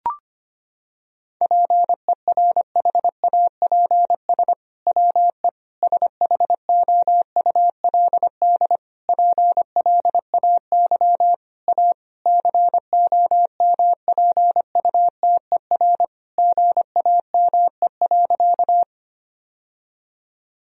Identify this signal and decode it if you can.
Morse code, heard as a keyed tone, about 25 wpm, 700 Hz. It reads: PERHAPS WE SHOULD PLAY A COMPUTER GAME.
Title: Sentences from Top 1000 Words - Review Part 3 - 25wpm